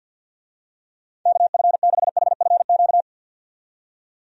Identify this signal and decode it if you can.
Morse code, heard as a keyed tone, about 52 wpm, 700 Hz. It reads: KF6HFQ
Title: USA Call Signs - 6 Characters - 50wpm